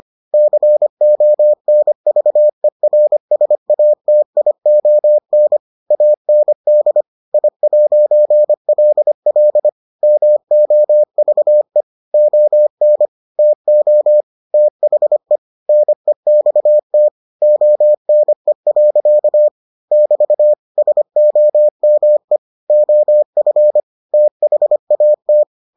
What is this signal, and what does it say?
Morse code, heard as a keyed tone, about 25 wpm, 600 Hz. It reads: CONVERSATION AND I'LL MOVE ON TO THE NEXT ONE. <BT> SOME OF THAT